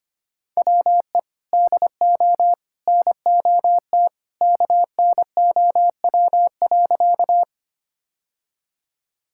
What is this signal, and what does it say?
Morse code, heard as a keyed tone, about 25 wpm, 700 Hz. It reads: WE DO NOT KNOW.
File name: practice / 25wpm / Sentences with 3-Word Phrases - 25wpm